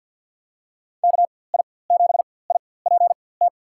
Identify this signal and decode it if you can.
Morse code, heard as a keyed tone, about 47 wpm, spaced as for 23 wpm, 700 Hz. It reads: KI7IPT
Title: USA Call Signs - 6 Characters - 4x Character Spacing - 50wpm